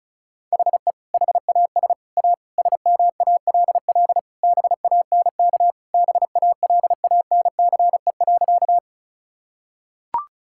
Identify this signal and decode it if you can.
Morse code, heard as a keyed tone, about 36 wpm, 700 Hz. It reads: HE HAS A SMALL BANK BALANCE.
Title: Sentences from Top 1000 Words - Review Part 6 - 35wpm